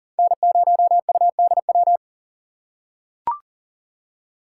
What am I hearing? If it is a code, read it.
Morse code, heard as a keyed tone, about 40 wpm, 700 Hz. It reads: N0UDW
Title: USA Call Signs - 5 Characters - 4x Character Spacing - 40wpm